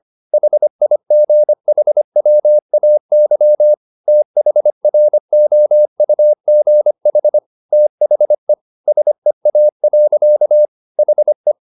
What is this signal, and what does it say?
Morse code, heard as a keyed tone, about 25 wpm, 600 Hz. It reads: HIGHWAY THROUGH THE SEA. HE